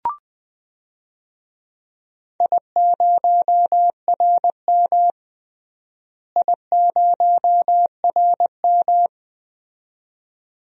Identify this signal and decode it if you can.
Morse code, heard as a keyed tone, about 20 wpm, 700 Hz. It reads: I0RM I0RM
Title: DXCC-100 Call Signs - 4 Characters - Repeated 2x - 20wpm